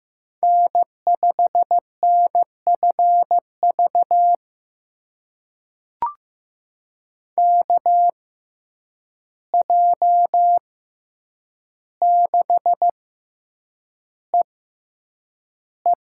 Morse code, heard as a keyed tone, about 15 wpm, spaced as for 8 wpm, 700 Hz. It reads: N5NFV KJ6EE
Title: USA Call Signs - 5 Characters - 6x Character Spacing - 15wpm